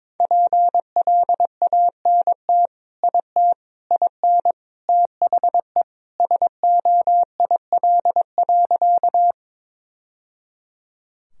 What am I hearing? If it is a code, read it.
Morse code, heard as a keyed tone, about 21 wpm, 700 Hz. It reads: PLANT IT IN THE SOIL.
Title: Sentences from Top 1000 Words - Review Part 4 - 22wpm